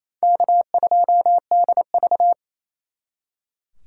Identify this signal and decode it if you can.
Morse code, heard as a keyed tone, about 36 wpm, 700 Hz. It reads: K2DV